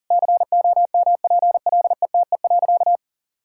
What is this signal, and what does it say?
Morse code, heard as a keyed tone, about 40 wpm, 700 Hz. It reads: COMPLETE.
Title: Sentences from Top 1000 Words - Review Part 6 - 40wpm